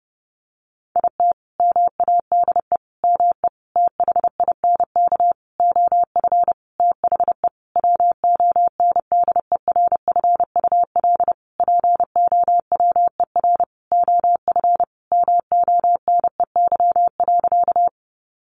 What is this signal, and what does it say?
Morse code, heard as a keyed tone, about 30 wpm, 700 Hz. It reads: IT MADE ME THINK OF THE WONDERFUL POWER OF MONEY.
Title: Sentences from Top 1100 Words 30wpm